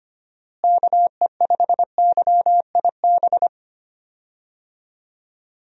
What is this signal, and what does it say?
Morse code, heard as a keyed tone, about 25 wpm, 700 Hz. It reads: KE5YIB
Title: USA Call Signs - 6 Characters - Repeated 3x - 25wpm